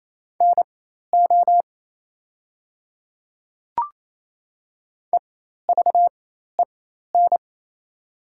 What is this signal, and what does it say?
Morse code, heard as a keyed tone, about 27 wpm, spaced as for 13 wpm, 700 Hz. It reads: NO EVEN
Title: 4x Character Spacing - Top 200 Words - 28wpm